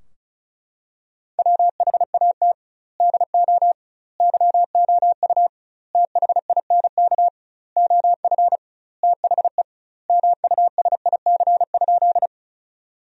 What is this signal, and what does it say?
Morse code, heard as a keyed tone, about 36 wpm, 700 Hz. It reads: WHAT DO YOU THINK OF THE MUSIC?
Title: 2x Word-Spacing - Sentences from Top 300 Words - 35wpm